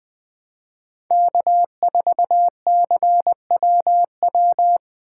Morse code, heard as a keyed tone, about 20 wpm, 700 Hz. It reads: K4CWW
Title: USA Call Signs - 5 Characters - 2x Character Spacing - 20wpm